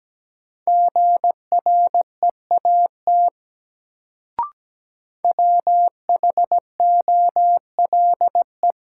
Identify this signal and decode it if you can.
Morse code, heard as a keyed tone, about 17 wpm, 700 Hz. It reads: GREAT WHOLE
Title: Sets of 2 Words - New Word Drill - 17wpm